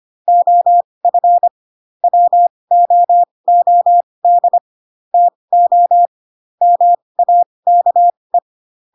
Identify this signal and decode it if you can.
Morse code, heard as a keyed tone, about 25 wpm, spaced as for 20 wpm, 700 Hz. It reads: OF WOOD TO MAKE
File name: Pinocchio / Pinocchio ch2 15 25 ik1hge